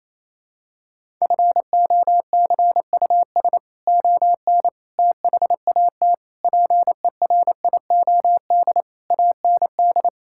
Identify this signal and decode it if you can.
Morse code, heard as a keyed tone, about 28 wpm, 700 Hz. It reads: FOCUS ON THAT PERIOD AND
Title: Sentences from Top 1000 Words - Review Part 5 - 28wpm